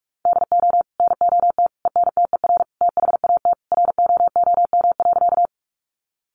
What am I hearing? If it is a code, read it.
Morse code, heard as a keyed tone, about 44 wpm, 700 Hz. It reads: DO NOT ENTER THAT ROOM.
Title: Sentences from Top 1300 Words 45wpm